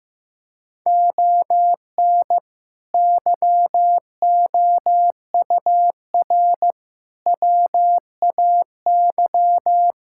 Morse code, heard as a keyed tone, about 16 wpm, 700 Hz. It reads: ON YOUR WAY
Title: Sentences from Top 1000 Words - Review Part 6 - 15wpm